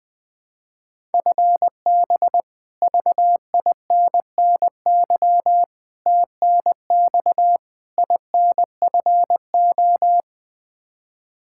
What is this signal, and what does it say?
Morse code, heard as a keyed tone, about 20 wpm, 700 Hz. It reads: FB VINNY TNX INFO
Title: Practice QSO - Exchange 3 - More Info - 20wpm